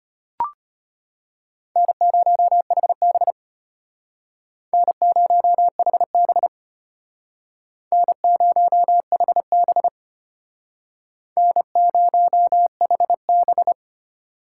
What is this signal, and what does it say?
Morse code, heard as a keyed tone, about 32 wpm, 700 Hz. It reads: N0HB N0HB N0HB N0HB